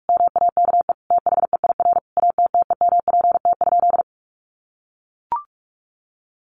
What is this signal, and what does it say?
Morse code, heard as a keyed tone, about 44 wpm, 700 Hz. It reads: MAKE THEIR ATTEMPT?
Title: Sentences from Top 1000 Words 45wpm